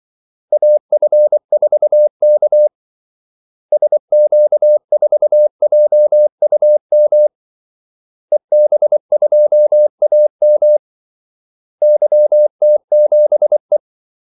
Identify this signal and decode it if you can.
Morse code, heard as a keyed tone, about 24 wpm, 600 Hz. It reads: AF4K SQ4JUM EB2AM YT7E
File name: Callsigns-24wpm